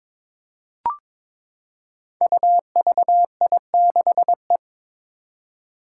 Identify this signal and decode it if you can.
Morse code, heard as a keyed tone, about 23 wpm, 700 Hz. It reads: UVI6E
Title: Code Groups - 5 Letter:Numbers - 22wpm